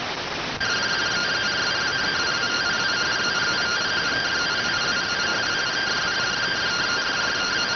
Two-tone teleprinter signal, about 170 Hz shift.